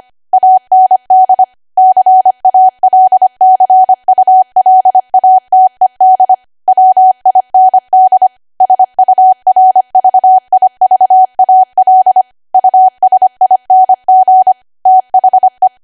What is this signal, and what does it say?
Morse code, heard as a keyed tone, about 25 wpm, 750 Hz. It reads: AND CALCULATED WIND SURVIVAL USING THE
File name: Archive / 25 WPM / 25WPM